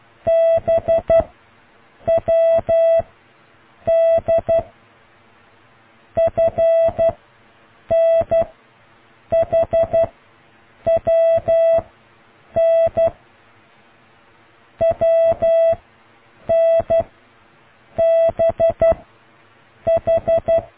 Morse code, heard as a keyed tone, about 12 wpm, 660 Hz. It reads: B W D F N H W N W N B H